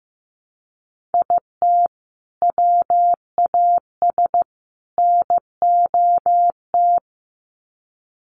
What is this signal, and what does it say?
Morse code, heard as a keyed tone, about 15 wpm, 700 Hz. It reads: IT WAS NOT
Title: Sets of 3 Words 15wpm